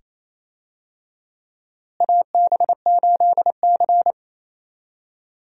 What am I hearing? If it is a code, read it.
Morse code, heard as a keyed tone, about 28 wpm, 700 Hz. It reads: AB8C
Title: USA Call Signs - 4 Characters - Repeated 2x - 28wpm